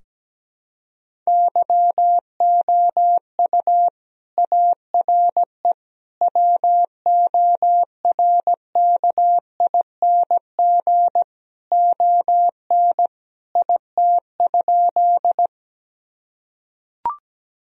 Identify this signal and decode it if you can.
Morse code, heard as a keyed tone, about 17 wpm, 700 Hz. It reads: YOU ARE WORKING ON IT?
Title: Sentences from Top 1000 Words - Review Part 4 - 17wpm